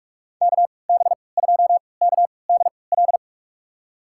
Morse code, heard as a keyed tone, about 46 wpm, spaced as for 26 wpm, 700 Hz. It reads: KB2KDL
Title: USA Call Signs - 6 Characters - 3x Character Spacing - 45wpm